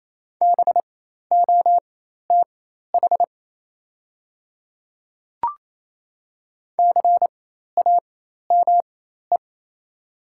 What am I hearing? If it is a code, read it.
Morse code, heard as a keyed tone, about 28 wpm, spaced as for 13 wpm, 690 Hz. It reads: BOTH CAME
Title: 4x Character Spacing - Top 300 Words - 28wpm